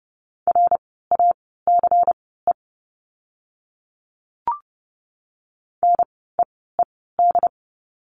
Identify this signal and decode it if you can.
Morse code, heard as a keyed tone, about 30 wpm, spaced as for 17 wpm, 700 Hz. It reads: RACE NEED
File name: Lesson 16f - Letter c - Words 30wpm